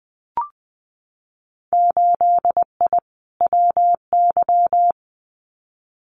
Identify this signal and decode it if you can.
Morse code, heard as a keyed tone, about 20 wpm, 700 Hz. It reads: 8I WY
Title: Winter Field Day Exchanges Encore 20wpm